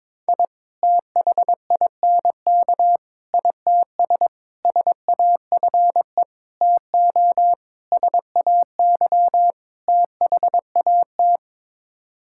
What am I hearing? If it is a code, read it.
Morse code, heard as a keyed tone, about 22 wpm, 700 Hz. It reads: I THINK ITS SAFE TO SAY THAT
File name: practice / 22wpm / Sets of 7 Words - 22wpm